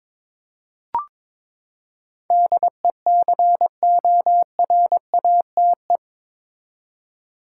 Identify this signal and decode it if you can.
Morse code, heard as a keyed tone, about 22 wpm, 700 Hz. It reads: DECORATE
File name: practice / 22wpm / Eight Letter Words - 22wpm